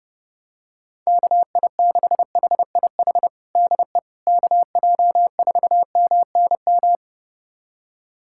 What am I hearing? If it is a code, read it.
Morse code, heard as a keyed tone, about 31 wpm, 700 Hz. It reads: KI6HIH DE KJ4MNM
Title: USA Callsign DE Callsign 30wpm